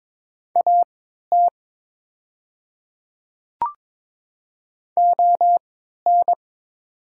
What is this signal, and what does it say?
Morse code, heard as a keyed tone, about 22 wpm, spaced as for 12 wpm, 700 Hz. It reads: AT ON
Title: 3x Character Spacing - Top 100 Words - 22wpm